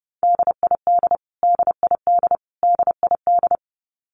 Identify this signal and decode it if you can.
Morse code, heard as a keyed tone, about 30 wpm, 700 Hz. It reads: DID DID DID
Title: Top 300 Words Repeated 30wpm